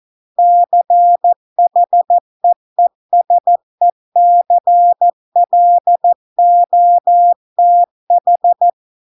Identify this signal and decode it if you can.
Morse code, heard as a keyed tone, about 14 wpm, 700 Hz. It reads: CHEESECLOTH